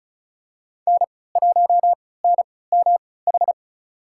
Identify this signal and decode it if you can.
Morse code, heard as a keyed tone, about 34 wpm, spaced as for 20 wpm, 700 Hz. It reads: N1NMH
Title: USA Call Signs - 5 Characters - 3x Character Spacing - 35wpm